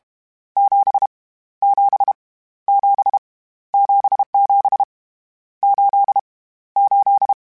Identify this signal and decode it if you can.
Morse code, heard as a keyed tone, about 31 wpm, 800 Hz. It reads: 7 7 7 77 8 8